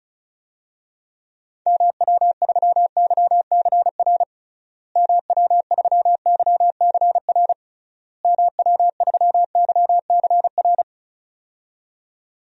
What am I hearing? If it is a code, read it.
Morse code, heard as a keyed tone, about 34 wpm, 700 Hz. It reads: MW3YCR MW3YCR MW3YCR
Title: DXCC-100 Call Signs - 6 Characters - Repeated 3x - 35wpm